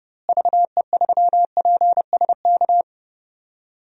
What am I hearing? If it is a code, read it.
Morse code, heard as a keyed tone, about 30 wpm, 700 Hz. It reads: VE3PSK
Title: DXCC-100 Call Signs - 6 Characters - Repeated 3x - 30wpm